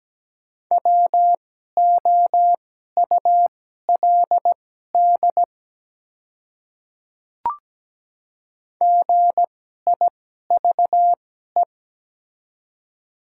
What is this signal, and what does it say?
Morse code, heard as a keyed tone, about 17 wpm, spaced as for 12 wpm, 700 Hz. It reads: WOULD GIVE
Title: 2x Character Spacing - Top 200 Words - 17wpm